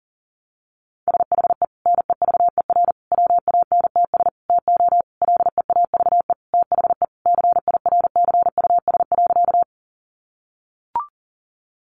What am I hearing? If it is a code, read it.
Morse code, heard as a keyed tone, about 40 wpm, 700 Hz. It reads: SHE NEVER WANTS TO LEAVE THE CIRCUS.
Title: Easy Phrases 40wpm